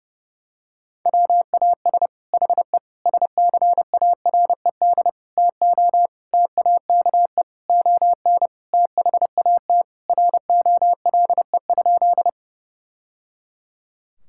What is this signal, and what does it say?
Morse code, heard as a keyed tone, about 30 wpm, 700 Hz. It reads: WAS HE SCARED TO TAKE ON THAT ROLE?
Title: Sentences from Top 1000 Words - Review Part 6 - 30wpm